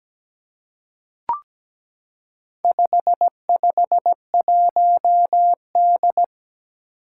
Morse code, heard as a keyed tone, about 17 wpm, 700 Hz. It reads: 551D
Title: Four Challenging Letter-Number Combinations 17wpm